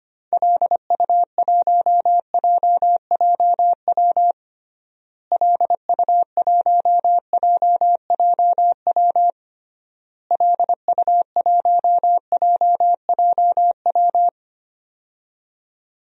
Morse code, heard as a keyed tone, about 25 wpm, 700 Hz. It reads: LU1JJW LU1JJW LU1JJW
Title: DXCC-100 Call Signs - 6 Characters - Repeated 3x - 25wpm